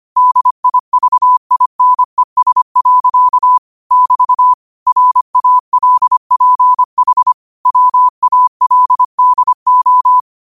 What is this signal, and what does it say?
Morse code, heard as a keyed tone, about 25 wpm, 1000 Hz. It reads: DIVINES. <BT> RALPH WALDO
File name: qotdfiles / 25wpm